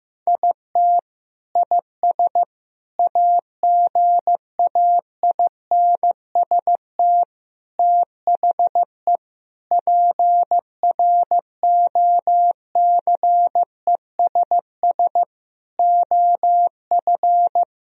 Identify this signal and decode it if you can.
Morse code, heard as a keyed tone, about 15 wpm, 700 Hz. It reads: IT IS AGAINST THE PROCESS OF
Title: Sentences from Top 1000 Words - Review Part 3 - 15wpm